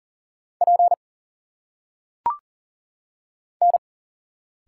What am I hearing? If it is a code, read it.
Morse code, heard as a keyed tone, about 38 wpm, 700 Hz. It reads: P N
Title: Single Letters - Phonetic Pronunciation 40wpm